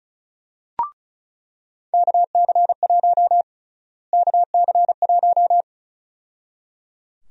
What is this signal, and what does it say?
Morse code, heard as a keyed tone, about 36 wpm, 700 Hz. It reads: KC1 KC1
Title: USA Call Sign Prefixes - Repeated 2x - 35wpm